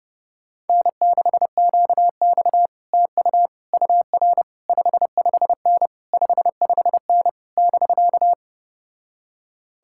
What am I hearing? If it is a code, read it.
Morse code, heard as a keyed tone, about 33 wpm, 700 Hz. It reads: N6QX TU UR 55N 55N <BK>